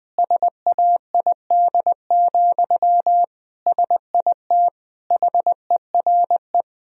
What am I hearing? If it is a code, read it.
Morse code, heard as a keyed tone, about 20 wpm, 700 Hz. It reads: SAID, SIT HERE
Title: Sentences from Top 1000 Words - Review Part 1 - 20wpm